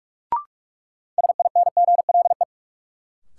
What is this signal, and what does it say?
Morse code, heard as a keyed tone, about 43 wpm, 700 Hz. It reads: SINGLE
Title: Sentences from Top 700 Words - New Word Drill - 45wpm